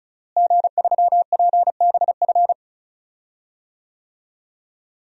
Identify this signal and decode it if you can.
Morse code, heard as a keyed tone, about 36 wpm, 700 Hz. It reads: G3PBF
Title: DXCC-100 Call Signs - 5 Characters - Repeated 3x - 35wpm